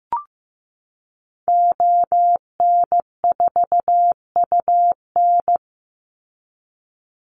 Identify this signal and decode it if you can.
Morse code, heard as a keyed tone, about 15 wpm, 700 Hz. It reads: ON4UN
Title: Call Signs Repeated 3x - 15wpm